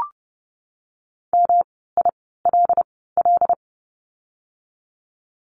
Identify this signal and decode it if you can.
Morse code, heard as a keyed tone, about 30 wpm, spaced as for 17 wpm, 700 Hz. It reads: MILL
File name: course / 30:10wpm / 108 Lesson 18f - Letter m - Words 30wpm